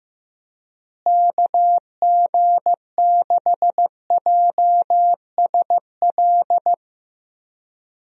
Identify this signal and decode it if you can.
Morse code, heard as a keyed tone, about 15 wpm, 700 Hz. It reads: KG6JSL
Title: USA Call Signs - 6 Characters - Repeated 3x - 15wpm